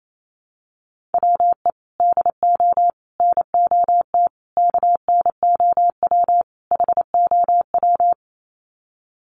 Morse code, heard as a keyed tone, about 28 wpm, 690 Hz. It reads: WE DO NOT KNOW HOW
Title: Sets of 5 Words Encore 28wpm